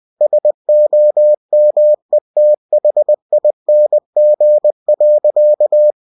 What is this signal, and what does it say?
Morse code, heard as a keyed tone, about 20 wpm, 600 Hz. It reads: SOMETHING.
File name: cw-books / winnie-the-pooh / winnie-20wpm-page-0034